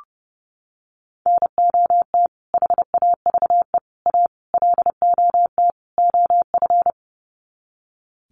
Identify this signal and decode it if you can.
Morse code, heard as a keyed tone, about 30 wpm, 700 Hz. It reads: NOT HAVE A LOT OF
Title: Sets of 5 Words 30wpm